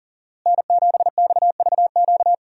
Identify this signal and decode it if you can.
Morse code, heard as a keyed tone, about 40 wpm, 700 Hz. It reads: N7XVQ
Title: USA Call Signs - 5 Characters - Repeated 2x - 40wpm